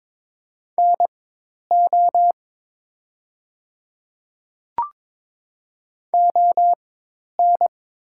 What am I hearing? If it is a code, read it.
Morse code, heard as a keyed tone, about 23 wpm, spaced as for 10 wpm, 700 Hz. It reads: NO ON
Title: 4x Character Spacing - Top 100 Words - 22wpm